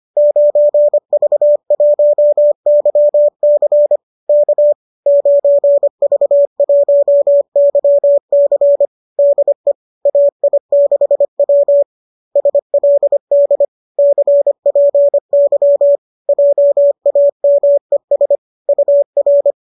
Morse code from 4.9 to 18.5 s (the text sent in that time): 9V1YC DE AI6W SLD CPY JAMES